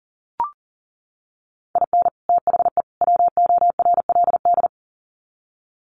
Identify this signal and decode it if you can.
Morse code, heard as a keyed tone, about 40 wpm, 700 Hz. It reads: IN THE WORLD
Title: Sets of 3 Words 40wpm